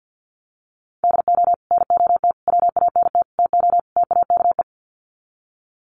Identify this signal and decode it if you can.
Morse code, heard as a keyed tone, about 50 wpm, 700 Hz. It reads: DO NOT WANT TO TAKE
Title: Sets of 5 Words Encore2 50wpm